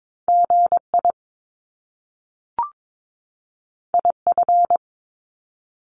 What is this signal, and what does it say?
Morse code, heard as a keyed tone, about 22 wpm, 700 Hz. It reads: GI IF
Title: Top Two Letter Combinations in Words 22wpm